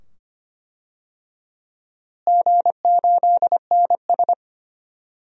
Morse code, heard as a keyed tone, about 25 wpm, 700 Hz. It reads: G8NS
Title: DXCC-100 Call Signs - 4 Characters - Repeated 2x - 25wpm